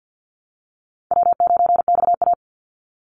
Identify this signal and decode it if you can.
Morse code, heard as a keyed tone, about 50 wpm, 700 Hz. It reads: W9XA
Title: Call Signs Repeated 2x - 50wpm